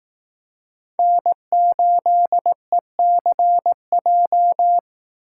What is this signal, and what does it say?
Morse code, heard as a keyed tone, about 18 wpm, 700 Hz. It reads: N8ECJ
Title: Speed Racing - USA 1x3 Call Signs 15wpm